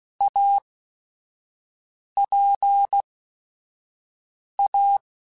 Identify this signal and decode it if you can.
Morse code, heard as a keyed tone, about 16 wpm, spaced as for 5 wpm, 800 Hz. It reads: APA
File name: Technical Subjects / Technical Subjects-mp3s / XC @5